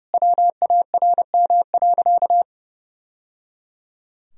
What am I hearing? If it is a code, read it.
Morse code, heard as a keyed tone, about 30 wpm, 700 Hz. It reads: WARM.
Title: Sentences from Top 1000 Words - Review Part 4 - 30wpm